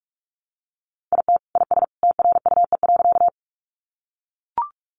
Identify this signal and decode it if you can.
Morse code, heard as a keyed tone, about 44 wpm, 700 Hz. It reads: IT IS TRUE.
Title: Sentences from Top 400 Words 45wpm